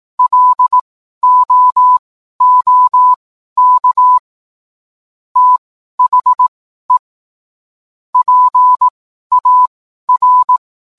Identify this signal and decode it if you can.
Morse code, heard as a keyed tone, about 18 wpm, spaced as for 12 wpm, 1000 Hz. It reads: LOOK THE PAR